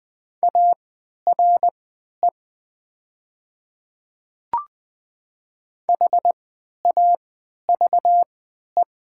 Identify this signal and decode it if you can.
Morse code, heard as a keyed tone, about 20 wpm, spaced as for 11 wpm, 700 Hz. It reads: ARE HAVE
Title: 3x Character Spacing - Top 100 Words - 20wpm